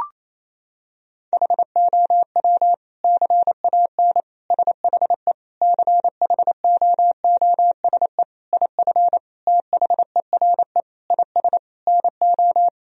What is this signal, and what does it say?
Morse code, heard as a keyed tone, about 28 wpm, 700 Hz. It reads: HOW CAN SHE CHOOSE IF THERE IS NO
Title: Sentences from Top 1000 Words - Review Part 4 - 28wpm